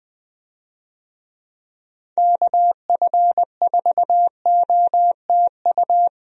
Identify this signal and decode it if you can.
Morse code, heard as a keyed tone, about 20 wpm, 700 Hz. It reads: KF4OTU
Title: USA Call Signs - 6 Characters - 20wpm